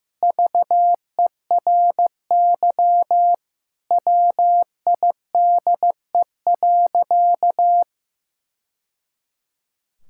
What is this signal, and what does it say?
Morse code, heard as a keyed tone, about 15 wpm, 700 Hz. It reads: VERY WIDE.
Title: Sentences from Top 1000 Words - Review Part 6 - 15wpm